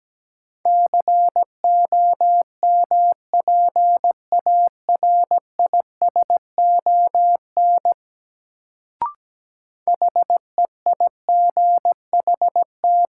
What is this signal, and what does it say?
Morse code, heard as a keyed tone, about 17 wpm, 700 Hz. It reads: COMPARISON HEIGHT
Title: Sentences from Top 1600 Words - New Word Drill - 17wpm